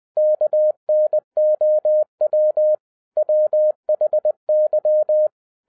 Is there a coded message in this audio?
Yes: Morse code from 0.2 to 5.3 s, heard as a keyed tone, about 20 wpm, 610 Hz. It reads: KNOW WHY